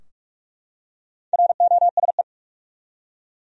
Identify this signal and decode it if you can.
Morse code, heard as a keyed tone, about 46 wpm, 700 Hz. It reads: ROSE
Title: USA Common Names - Max 4 Letters - 45wpm